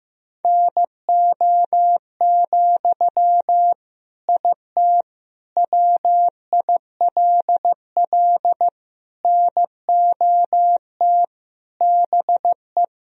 Morse code, heard as a keyed tone, about 15 wpm, 700 Hz. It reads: NO, IT WILL NOT BE
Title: Sentences from Top 1000 Words - Review Part 4 - 15wpm